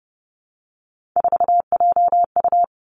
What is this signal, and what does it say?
Morse code, heard as a keyed tone, about 30 wpm, 700 Hz. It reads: 4JU
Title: Three Challenging Letter-Number Combinations 30wpm